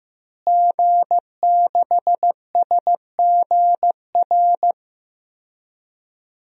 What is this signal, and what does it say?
Morse code, heard as a keyed tone, about 15 wpm, 700 Hz. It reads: G6SGR
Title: DXCC-100 Call Signs - 5 Characters - Repeated 3x - 15wpm